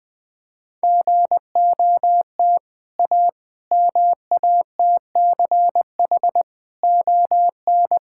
Morse code, heard as a keyed tone, about 20 wpm, 700 Hz. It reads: GOT A MATCH ON
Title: Sentences from Top 1600 Words 20wpm